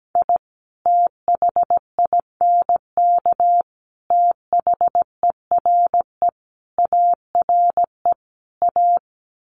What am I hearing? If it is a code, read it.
Morse code, heard as a keyed tone, about 17 wpm, 700 Hz. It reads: I THINK THERE ARE A
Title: Sets of 5 Words Encore2 17wpm